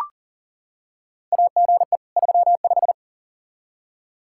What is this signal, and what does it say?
Morse code, heard as a keyed tone, about 40 wpm, 690 Hz. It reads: AGE 35
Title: Practice QSO - Exchange 3 - More Info - 40wpm